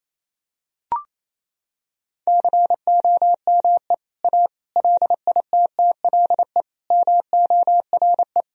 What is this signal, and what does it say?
Morse code, heard as a keyed tone, about 27 wpm, 700 Hz. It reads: COME A LITTLE MORE
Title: Sentences from Top 1000 Words - Review Part 2 - 28wpm